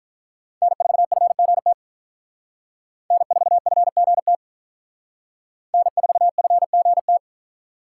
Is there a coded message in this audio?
Yes: Morse code from 0.6 to 7.2 s, heard as a keyed tone, about 48 wpm, 700 Hz. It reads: N4FGT N4FGT N4FGT